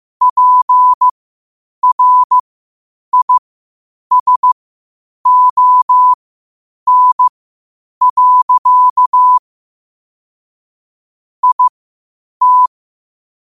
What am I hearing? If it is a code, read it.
Morse code, heard as a keyed tone, about 15 wpm, spaced as for 8 wpm, 1000 Hz. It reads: PRISON. IT
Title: 8 WPM morse code quotes for Thu, 29 Feb 2024